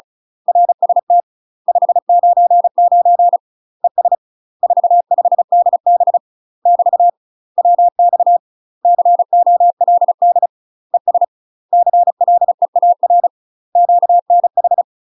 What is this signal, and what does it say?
Morse code, heard as a keyed tone, about 34 wpm, 700 Hz. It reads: RST 599 ES 45DB <BT> WX COLD ES CLEAR QNH
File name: UK MASTER QSOs TRAINING / 35WPM / UK MASTER QSOs 0000